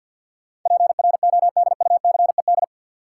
Morse code, heard as a keyed tone, about 50 wpm, 700 Hz. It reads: PRODUCED
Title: Sentences from Top 1000 Words - Review Part 6 - New Word Drill - 50wpm